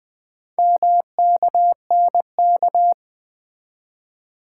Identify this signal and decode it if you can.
Morse code, heard as a keyed tone, about 20 wpm, 700 Hz. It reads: MKNK
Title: Four Letter-Number Combinations - 20wpm